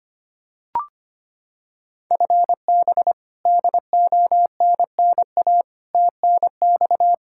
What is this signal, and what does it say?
Morse code, heard as a keyed tone, about 25 wpm, 700 Hz. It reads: FB DONNA TNX
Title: Practice QSO - Exchange 3 - More Info - 25wpm